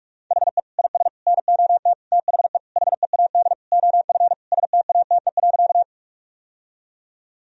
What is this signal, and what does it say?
Morse code, heard as a keyed tone, about 42 wpm, 700 Hz. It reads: HE IS NOT THE HEAD OF STATE.
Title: Sentences from Top 1000 Words - Review Part 2 - 45wpm